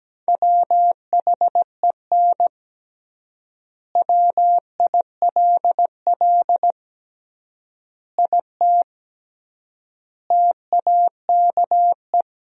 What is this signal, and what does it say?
Morse code, heard as a keyed tone, about 17 wpm, 700 Hz. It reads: WHEN WILL IT TAKE
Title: 3x Word-Spacing - Sentences from Top 200 Words - 17wpm